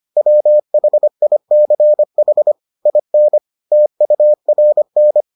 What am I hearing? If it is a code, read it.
Morse code, heard as a keyed tone, about 25 wpm, 600 Hz. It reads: WHICH IN TURN